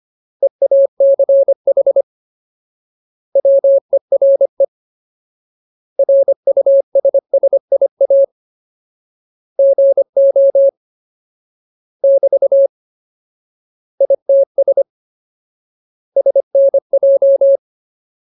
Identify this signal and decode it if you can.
Morse code, heard as a keyed tone, about 25 wpm, 550 Hz. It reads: EACH WERE RUSSIA GO <BT> ITS SNJ
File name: Ultimate Word List / 25WPM Ultimate